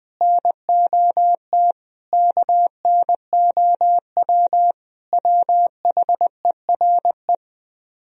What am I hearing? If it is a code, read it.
Morse code, heard as a keyed tone, about 20 wpm, 700 Hz. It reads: NOT KNOW WHERE
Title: Speed-Racing - Sets of 5 Words 17wpm